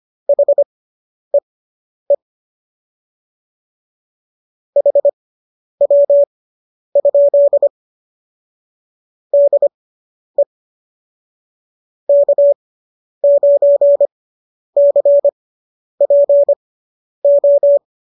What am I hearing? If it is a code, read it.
Morse code, heard as a keyed tone, about 25 wpm, spaced as for 10 wpm, 570 Hz. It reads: HEE HW? DE K9CPO